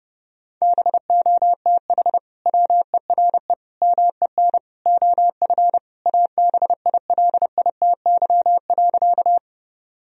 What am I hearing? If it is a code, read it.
Morse code, heard as a keyed tone, about 32 wpm, 700 Hz. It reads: BOTH WERE MEN OF ABILITY.
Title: Sentences from Top 1000 Words - Review Part 3 - 30wpm